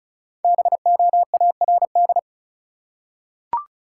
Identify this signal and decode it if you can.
Morse code, heard as a keyed tone, about 34 wpm, 700 Hz. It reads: BOARD
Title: Five Letter Words - 35wpm